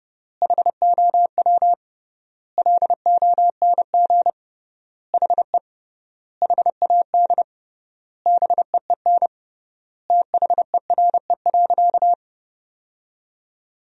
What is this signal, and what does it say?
Morse code, heard as a keyed tone, about 30 wpm, 700 Hz. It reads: HOW LONG HE HAD BEEN THERE.
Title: 3x Word-Spacing - Sentences from Top 300 Words - 30wpm